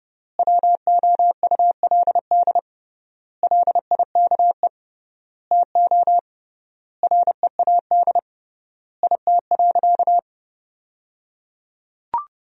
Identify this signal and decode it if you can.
Morse code, heard as a keyed tone, about 29 wpm, 700 Hz. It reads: WOULD LIKE TO READ IT.
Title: 3x Word-Spacing - Sentences from Top 500 Words - 30wpm